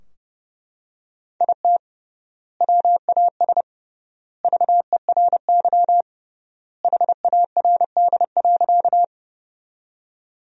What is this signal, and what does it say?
Morse code, heard as a keyed tone, about 30 wpm, 700 Hz. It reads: IT WAS VERY HARD.
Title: 3x Word-Spacing - Sentences from Top 300 Words - 30wpm